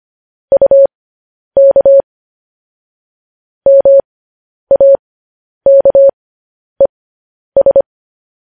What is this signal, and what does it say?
Morse code, heard as a keyed tone, about 25 wpm, spaced as for 10 wpm, 550 Hz. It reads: UK MAKES
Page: news-headlines-in-morse-code-at-10-wpm